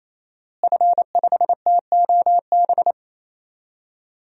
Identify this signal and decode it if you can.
Morse code, heard as a keyed tone, about 28 wpm, 700 Hz. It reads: F5TOB
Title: DXCC-100 Call Signs - 5 Characters - Repeated 3x - 28wpm